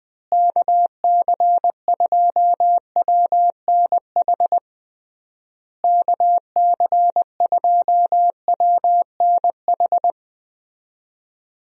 Morse code, heard as a keyed tone, about 20 wpm, 700 Hz. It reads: KC2WNH KC2WNH
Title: USA Call Signs - 6 Characters - Repeated 3x - 20wpm